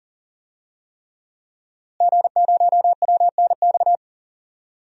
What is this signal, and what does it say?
Morse code, heard as a keyed tone, about 40 wpm, 700 Hz. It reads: G0WNX
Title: DXCC-100 Call Signs - 5 Characters - Repeated 3x - 40wpm